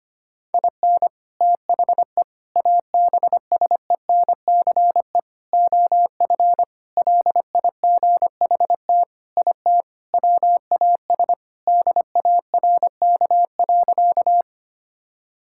Morse code, heard as a keyed tone, about 25 wpm, 700 Hz. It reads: IN THE ABSENCE OF LIGHT IT WAS DARK.
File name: practice / 25wpm / Sentences with 4-Word Phrases - 25wpm